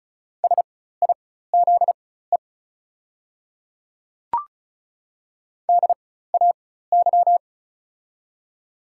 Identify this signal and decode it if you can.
Morse code, heard as a keyed tone, about 36 wpm, spaced as for 16 wpm, 690 Hz. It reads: SIZE DAY
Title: 4x Character Spacing - Top 500 Words - 35wpm